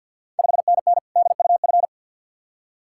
Morse code, heard as a keyed tone, about 51 wpm, 700 Hz. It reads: 5NN DUF
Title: Ontario QSO Party Exchange 50wpm